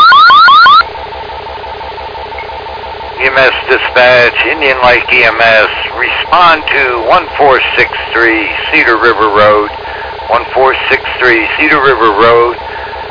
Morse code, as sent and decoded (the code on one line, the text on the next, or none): none